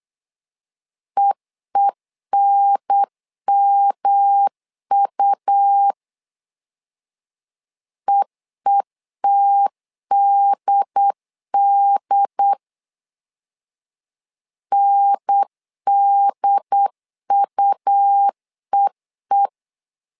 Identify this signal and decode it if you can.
Morse code, heard as a keyed tone, about 8 wpm, 800 Hz. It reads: EENMU EETDD NDUEE
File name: LEZIONI / 3 MWRO / 3 M 8wpm 2300